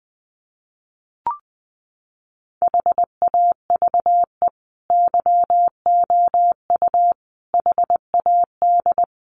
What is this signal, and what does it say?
Morse code, heard as a keyed tone, about 20 wpm, 700 Hz. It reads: HAVE YOU HAD
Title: Sentences from Top 1100 Words 20wpm